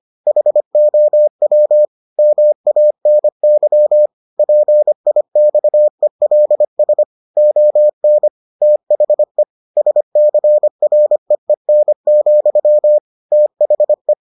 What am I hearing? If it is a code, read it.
Morse code, heard as a keyed tone, about 25 wpm, 600 Hz. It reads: HOW MANY PIXELS ON THE SCREEN, THE